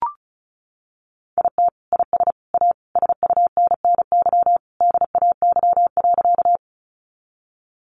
Morse code, heard as a keyed tone, about 34 wpm, 700 Hz. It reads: IT IS A SUNNY DAY.
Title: Additional Easy Phrases 35wpm